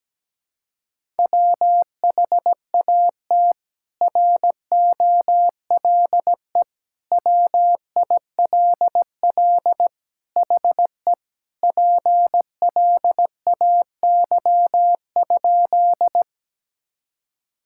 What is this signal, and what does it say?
Morse code, heard as a keyed tone, about 17 wpm, 700 Hz. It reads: WHAT ROLE WILL HE PLAY?
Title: Sentences from Top 1000 Words - Review Part 6 - 17wpm